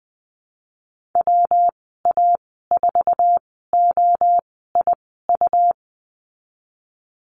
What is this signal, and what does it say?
Morse code, heard as a keyed tone, about 19 wpm, spaced as for 14 wpm, 700 Hz. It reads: WA4OIU
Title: 147 Lesson 24g - Letter y - Call Signs 20wpm